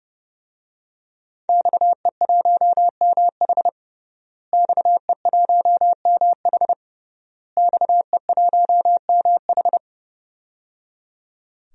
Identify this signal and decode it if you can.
Morse code, heard as a keyed tone, about 30 wpm, 700 Hz. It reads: XE1MH XE1MH XE1MH
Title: DXCC-100 Call Signs - 5 Characters - Repeated 3x - 30wpm